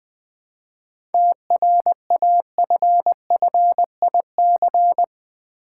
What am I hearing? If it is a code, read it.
Morse code, heard as a keyed tone, about 20 wpm, 700 Hz. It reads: TRAFFIC
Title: Easy Words with Hints - 20wpm